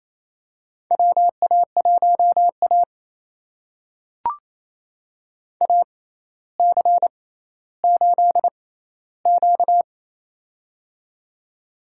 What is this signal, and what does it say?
Morse code, heard as a keyed tone, about 28 wpm, spaced as for 14 wpm, 700 Hz. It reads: WA1A AC8Q
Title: USA Call Signs - 4 Characters - 6x Character Spacing - 28wpm